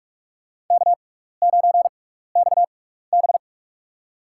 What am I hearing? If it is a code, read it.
Morse code, heard as a keyed tone, about 44 wpm, spaced as for 15 wpm, 700 Hz. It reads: K9XB
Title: USA Call Signs - 4 Characters - 6x Character Spacing - 45wpm